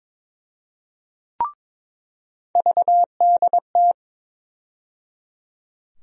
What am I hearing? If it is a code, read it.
Morse code, heard as a keyed tone, about 22 wpm, 700 Hz. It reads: VDT
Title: USA Call Sign Postfixes - 3 Letters 22wpm